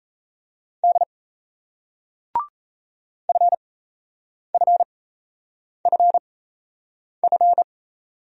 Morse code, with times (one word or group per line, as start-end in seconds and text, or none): 0.8-1.0 D
3.3-7.6 FFFF